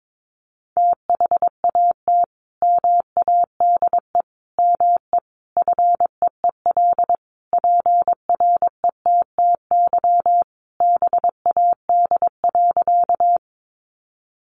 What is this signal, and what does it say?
Morse code, heard as a keyed tone, about 22 wpm, 700 Hz. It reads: THAT MADE ME FEEL PRETTY BAD.
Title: Sentences from Top 1000 Words 22wpm